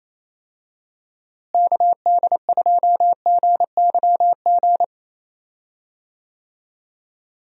Morse code, heard as a keyed tone, about 28 wpm, 700 Hz. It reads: KD2GYG
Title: USA Call Signs - 6 Characters - Repeated 2x - 28wpm